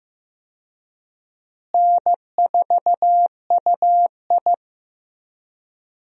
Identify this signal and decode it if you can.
Morse code, heard as a keyed tone, about 15 wpm, 700 Hz. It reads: N4UI
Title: USA Call Signs - 4 Characters - Repeated 2x - 15wpm